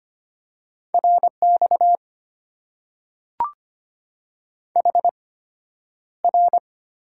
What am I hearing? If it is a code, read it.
Morse code, heard as a keyed tone, about 25 wpm, spaced as for 11 wpm, 700 Hz. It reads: RX HR
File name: Top 90 QSO Elements - 8x Character Spacing - 25wpm